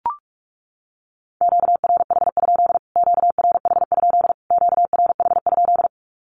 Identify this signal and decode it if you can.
Morse code, heard as a keyed tone, about 44 wpm, 700 Hz. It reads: QRH? QRH? QRH?
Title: Q-Codes Repeated 3x 45wpm